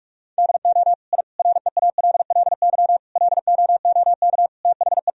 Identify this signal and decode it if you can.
Morse code, heard as a keyed tone, about 46 wpm, 700 Hz. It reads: DO I REALLY LOOK THE
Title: Sentences from Top 1000 Words - Review Part 2 - 45wpm